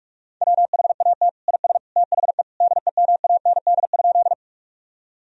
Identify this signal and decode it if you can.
Morse code, heard as a keyed tone, about 46 wpm, 700 Hz. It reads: WHAT IS THE DEMAND?
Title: Sentences from Top 1000 Words - Review Part 4 - 45wpm